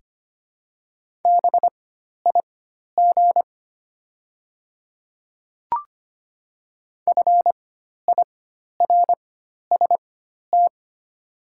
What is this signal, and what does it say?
Morse code, heard as a keyed tone, about 25 wpm, spaced as for 12 wpm, 690 Hz. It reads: BIG FIRST
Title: 4x Character Spacing - Top 200 Words - 25wpm